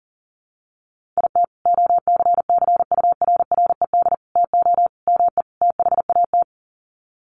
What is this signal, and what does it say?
Morse code, heard as a keyed tone, about 40 wpm, 700 Hz. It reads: IT OCCURRED TO ME THAT
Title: Sets of 5 Words Encore 40wpm